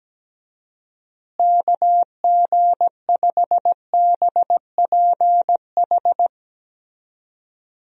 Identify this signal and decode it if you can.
Morse code, heard as a keyed tone, about 17 wpm, 700 Hz. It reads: KG5BPH
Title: USA Call Signs - 6 Characters - Repeated 3x - 17wpm